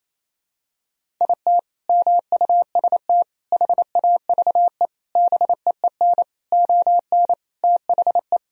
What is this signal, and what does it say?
Morse code, heard as a keyed tone, about 28 wpm, 700 Hz. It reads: IT MUST HAVE BEEN ON THE
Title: Sentences from Top 1000 Words - Review Part 1 - 28wpm